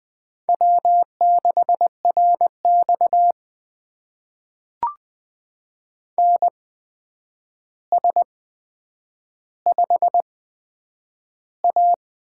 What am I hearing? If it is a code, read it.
Morse code, heard as a keyed tone, about 20 wpm, spaced as for 9 wpm, 700 Hz. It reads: W6RX NS5A
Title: USA Call Signs - 4 Characters - 8x Character Spacing - 20wpm